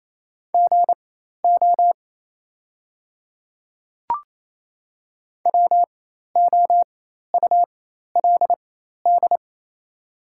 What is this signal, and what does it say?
Morse code, heard as a keyed tone, about 27 wpm, spaced as for 13 wpm, 690 Hz. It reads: GO WOULD
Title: 4x Character Spacing - Top 100 Words - 28wpm